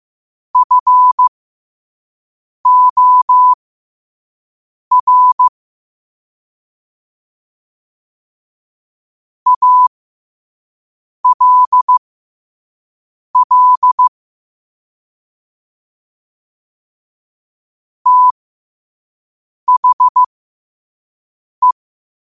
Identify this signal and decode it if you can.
Morse code, heard as a keyed tone, about 15 wpm, spaced as for 5 wpm, 1000 Hz. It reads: FOR ALL THE